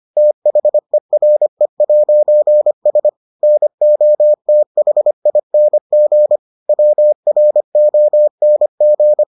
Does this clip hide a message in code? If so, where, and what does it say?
Yes: Morse code from 0.2 to 9.2 s, heard as a keyed tone, about 25 wpm, 600 Hz. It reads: THERE'S NOTHING WRONG